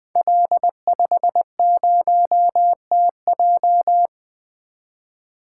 Morse code, heard as a keyed tone, about 20 wpm, 700 Hz. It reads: L50TJ
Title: Code Groups - 5 Letter:Numbers - 20wpm